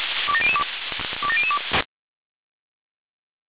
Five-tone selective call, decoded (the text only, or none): 27002 27002